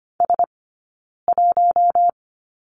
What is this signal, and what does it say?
Morse code, heard as a keyed tone, about 25 wpm, 700 Hz. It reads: S 1
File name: Lesson 15d - Number 5 - Two Characters 25wpm